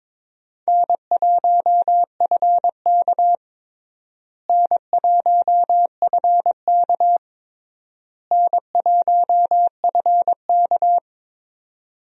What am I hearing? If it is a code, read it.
Morse code, heard as a keyed tone, about 22 wpm, 700 Hz. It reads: N1FK N1FK N1FK
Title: USA Call Signs - 4 Characters - Repeated 3x - 22wpm